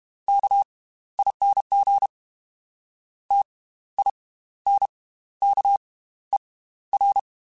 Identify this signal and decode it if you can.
Morse code, heard as a keyed tone, about 32 wpm, spaced as for 15 wpm, 800 Hz. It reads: KING TINKER